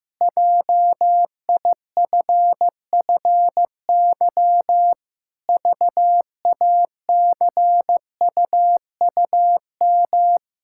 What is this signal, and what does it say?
Morse code, heard as a keyed tone, about 15 wpm, 700 Hz. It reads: JIFFY VACUUM
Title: Pangram Challenge v2 - 15wpm